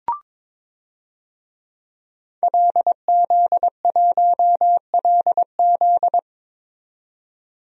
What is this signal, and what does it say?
Morse code, heard as a keyed tone, about 22 wpm, 700 Hz. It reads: LZ1LZ
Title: DXCC-100 Call Signs - 5 Characters - 22wpm